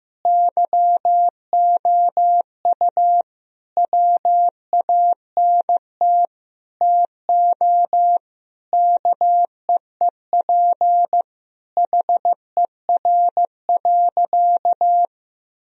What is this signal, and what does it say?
Morse code, heard as a keyed tone, about 15 wpm, 700 Hz. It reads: YOU WANT TO KEEP HER.
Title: Sentences from Top 1000 Words - Review Part 2 - 15wpm